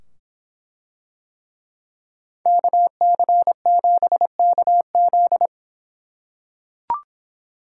Morse code, heard as a keyed tone, about 24 wpm, 700 Hz. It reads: KC7KZ